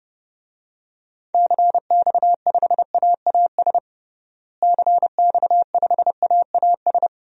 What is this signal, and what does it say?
Morse code, heard as a keyed tone, about 32 wpm, 700 Hz. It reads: CX5AAS CX5AAS
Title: DXCC-100 Call Signs - 6 Characters - Repeated 3x - 30wpm